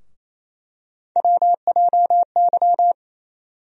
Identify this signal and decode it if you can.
Morse code, heard as a keyed tone, about 28 wpm, 700 Hz. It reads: WJY
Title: USA Call Sign Postfixes - 3 Letters 28wpm